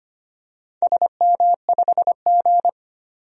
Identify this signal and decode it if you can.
Morse code, heard as a keyed tone, about 25 wpm, 700 Hz. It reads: SM5G
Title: DXCC-100 Call Signs - 4 Characters - 25wpm